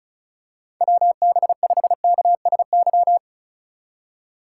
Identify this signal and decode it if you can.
Morse code, heard as a keyed tone, about 36 wpm, 700 Hz. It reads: WB5KSY
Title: USA Call Signs - 6 Characters - 4x Character Spacing - 35wpm